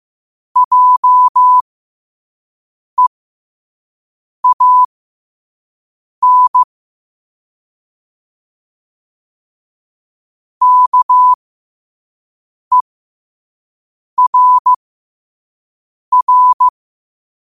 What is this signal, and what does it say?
Morse code, heard as a keyed tone, about 15 wpm, spaced as for 5 wpm, 1000 Hz. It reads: JEAN KERR
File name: qotdfiles / 5wpm